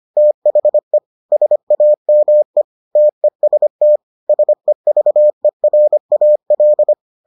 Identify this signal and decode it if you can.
Morse code, heard as a keyed tone, about 25 wpm, 600 Hz. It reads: THE SAME TEST SEVERAL